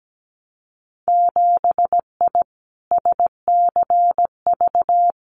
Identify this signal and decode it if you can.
Morse code, heard as a keyed tone, about 17 wpm, 700 Hz. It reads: 7I SCV
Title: Winter Field Day Exchanges Encore 17wpm